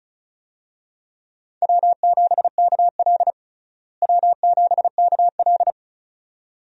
Morse code, heard as a keyed tone, about 34 wpm, 700 Hz. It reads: W7KL W7KL
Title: USA Call Signs - 4 Characters - Repeated 2x - 35wpm